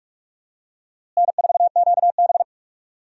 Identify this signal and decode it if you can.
Morse code, heard as a keyed tone, about 46 wpm, 700 Hz. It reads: N4QB
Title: USA Call Signs - 4 Characters - Repeated 2x - 45wpm